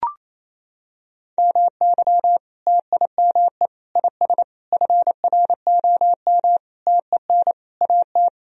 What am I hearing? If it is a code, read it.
Morse code, heard as a keyed tone, about 27 wpm, 700 Hz. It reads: MY TIME IS FROM TEN AT
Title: Sentences from Top 1000 Words - Review Part 2 - 28wpm